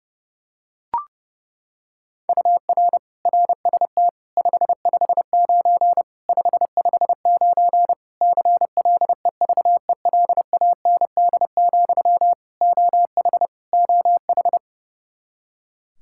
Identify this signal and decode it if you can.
Morse code, heard as a keyed tone, about 30 wpm, 700 Hz. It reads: UR RST 559 559 CLEVELAND, OH OH
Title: K1USN SST QSO Practice 30wpm